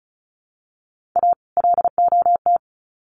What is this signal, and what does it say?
Morse code, heard as a keyed tone, about 34 wpm, 700 Hz. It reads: A LOT
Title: Sets of 2 Words 35wpm